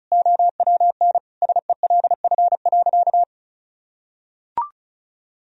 Morse code, heard as a keyed tone, about 34 wpm, 700 Hz. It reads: OWN SELF.
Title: Sentences from Top 1000 Words - Review Part 1 - 35wpm